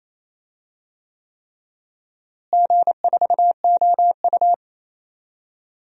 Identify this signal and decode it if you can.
Morse code, heard as a keyed tone, about 27 wpm, 700 Hz. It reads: G4OU
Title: DXCC-100 Call Signs - 4 Characters - Repeated 3x - 28wpm